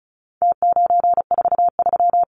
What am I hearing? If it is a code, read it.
Morse code, heard as a keyed tone, about 34 wpm, 700 Hz. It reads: T943
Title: Numbers - Contest Style - 4-Digit 35wpm